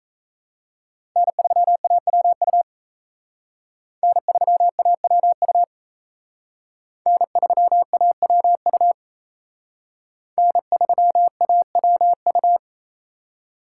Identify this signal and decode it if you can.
Morse code, heard as a keyed tone, about 32 wpm, 700 Hz. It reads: N3AWU N3AWU N3AWU N3AWU
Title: Speed Racing - USA 1x3 Call Signs 28wpm